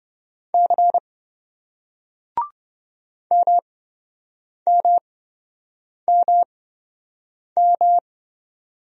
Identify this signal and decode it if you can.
Morse code, heard as a keyed tone, about 27 wpm, spaced as for 7 wpm, 700 Hz. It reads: C MMMM